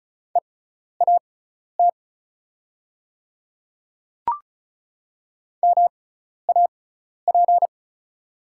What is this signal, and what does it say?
Morse code, heard as a keyed tone, about 36 wpm, spaced as for 12 wpm, 690 Hz. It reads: EAT MAP
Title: 6x Character Spacing - Top 500 Words - 35wpm